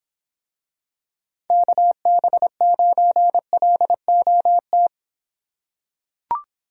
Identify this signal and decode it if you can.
Morse code, heard as a keyed tone, about 26 wpm, 700 Hz. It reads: KB9LOT